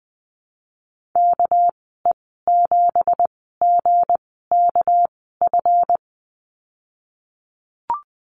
Morse code, heard as a keyed tone, about 19 wpm, spaced as for 14 wpm, 700 Hz. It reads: KE7GKF
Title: 213 Lesson 34g - Number 8 - Call Signs 20wpm